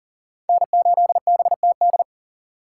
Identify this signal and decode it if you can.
Morse code, heard as a keyed tone, about 40 wpm, 700 Hz. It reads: N8BTD
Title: USA Call Signs - 5 Characters - 6x Character Spacing - 40wpm